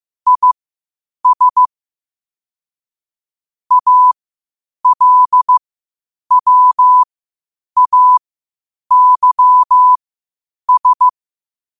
Morse code, heard as a keyed tone, about 15 wpm, spaced as for 8 wpm, 1000 Hz. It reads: IS ALWAYS
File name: qotdfiles / 8wpm